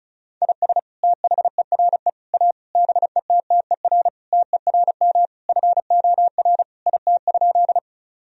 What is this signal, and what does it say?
Morse code, heard as a keyed tone, about 34 wpm, 700 Hz. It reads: IS THERE A BETTER TERM FOR IT?
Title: Sentences from Top 1000 Words - Review Part 4 - 35wpm